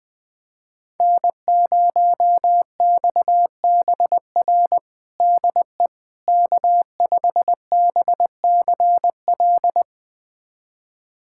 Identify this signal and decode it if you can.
Morse code, heard as a keyed tone, about 20 wpm, 700 Hz. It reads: N0XBR DE K5BCL
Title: USA Callsign DE Callsign - 5 Characters - 20wpm